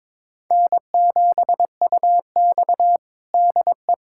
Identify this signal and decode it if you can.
Morse code, heard as a keyed tone, about 22 wpm, 700 Hz. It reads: N7UX DE
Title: USA Callsign DE Callsign - 4 Characters - 22wpm